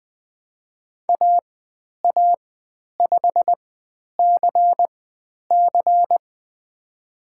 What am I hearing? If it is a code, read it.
Morse code, heard as a keyed tone, about 20 wpm, spaced as for 10 wpm, 700 Hz. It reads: AA5CC
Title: 095 Lesson 16g - Letter c - Call Signs 20wpm